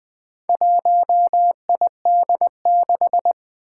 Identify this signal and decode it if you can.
Morse code, heard as a keyed tone, about 20 wpm, 700 Hz. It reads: 1ID6
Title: Four Challenging Letter-Number Combinations 20wpm